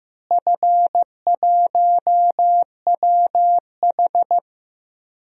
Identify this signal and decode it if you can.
Morse code, heard as a keyed tone, about 15 wpm, 700 Hz. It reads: F1WH
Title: DXCC-100 Call Signs - 4 Characters - Repeated 3x - 15wpm